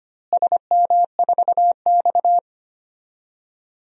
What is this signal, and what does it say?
Morse code, heard as a keyed tone, about 25 wpm, 700 Hz. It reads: SM4X
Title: DXCC-100 Call Signs - 4 Characters - Repeated 3x - 25wpm